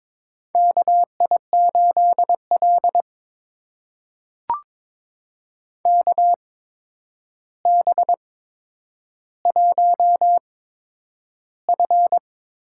Morse code, heard as a keyed tone, about 22 wpm, spaced as for 9 wpm, 700 Hz. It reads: KI8L KB1F
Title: USA Call Signs - 4 Characters - 8x Character Spacing - 22wpm